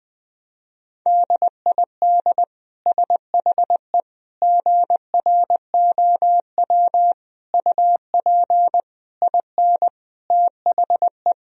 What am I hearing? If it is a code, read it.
Morse code, heard as a keyed tone, about 20 wpm, 700 Hz. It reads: DID SHE GROW UP IN THE
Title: Sentences from Top 1000 Words - Review Part 2 - 20wpm